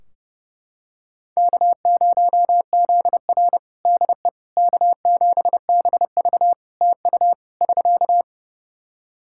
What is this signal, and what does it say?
Morse code, heard as a keyed tone, about 31 wpm, 700 Hz. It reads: K0ZR DE K7BV TU <SK>
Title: Practice QSO - Exchange 4 - Ending - 30wpm